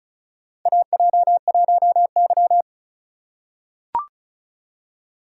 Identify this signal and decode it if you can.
Morse code, heard as a keyed tone, about 36 wpm, 700 Hz. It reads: AJ1Y
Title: USA Call Signs - 4 Characters - 4x Character Spacing - 35wpm